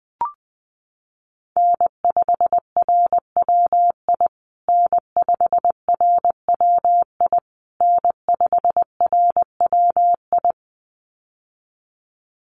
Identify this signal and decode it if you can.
Morse code, heard as a keyed tone, about 19 wpm, 700 Hz. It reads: N5RWI N5RWI N5RWI
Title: USA Call Signs Repeated 3x 20wpm